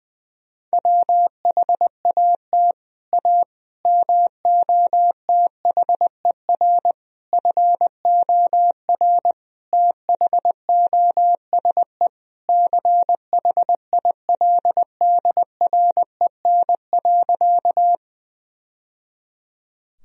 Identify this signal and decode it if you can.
Morse code, heard as a keyed tone, about 20 wpm, 700 Hz. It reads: WHAT A MOTHER FOR THOSE CHILDREN.
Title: Sentences from Top 1000 Words - Review Part 1 - 20wpm